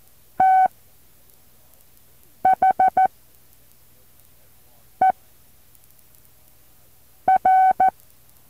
Morse code, heard as a keyed tone, about 14 wpm, spaced as for 4 wpm, 760 Hz. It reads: THER
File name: intl morse code2